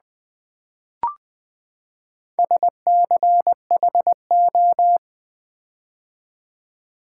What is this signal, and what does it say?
Morse code, heard as a keyed tone, about 20 wpm, 700 Hz. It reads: SCHO
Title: Build A Word - 4 Letters At A Time - 11-Letter Words - 20wpm